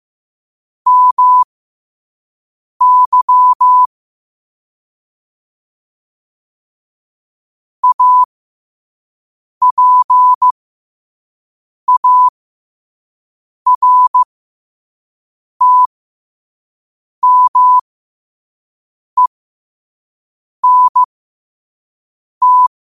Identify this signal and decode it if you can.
Morse code, heard as a keyed tone, about 15 wpm, spaced as for 5 wpm, 1000 Hz. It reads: MY APARTMENT
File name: qotdfiles / 5wpm